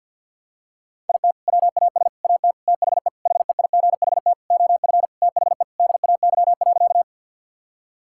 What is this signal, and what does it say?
Morse code, heard as a keyed tone, about 51 wpm, 700 Hz. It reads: IT WAS AT THE HEIGHT OF THE DAY.
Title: Sentences from Top 1600 Words 50wpm